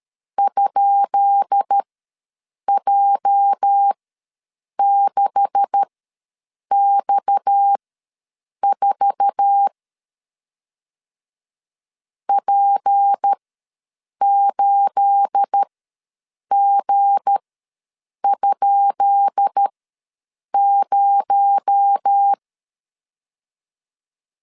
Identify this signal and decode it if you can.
Morse code, heard as a keyed tone, about 13 wpm, spaced as for 7 wpm, 800 Hz. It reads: ?J6X4 P8G?0
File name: LEZIONI / 10 SIGNS / 10B INT ws 12wpm 3527